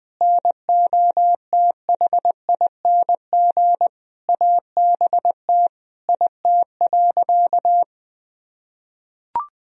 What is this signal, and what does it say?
Morse code, heard as a keyed tone, about 20 wpm, 700 Hz. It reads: NOTHING ABT IT.
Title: Sentences from Top 500 Words and Abbreviations 20wpm